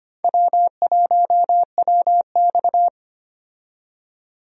Morse code, heard as a keyed tone, about 25 wpm, 700 Hz. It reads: W1WX
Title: USA Call Signs - 4 Characters - 25wpm